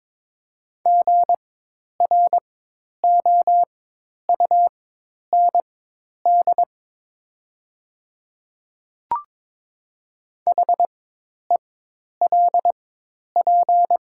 Morse code, heard as a keyed tone, about 22 wpm, spaced as for 10 wpm, 700 Hz. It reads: GROUND HELP
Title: 4x Character Spacing - Top 500 Words - 22wpm